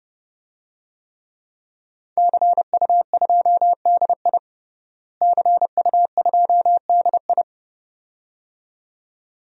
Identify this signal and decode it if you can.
Morse code, heard as a keyed tone, about 31 wpm, 700 Hz. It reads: CU2DI CU2DI
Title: DXCC-100 Call Signs - 5 Characters - Repeated 2x - 30wpm